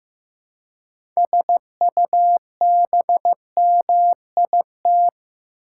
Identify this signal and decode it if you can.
Morse code, heard as a keyed tone, about 16 wpm, 700 Hz. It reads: SUBMIT
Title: Six Letter Words - 15wpm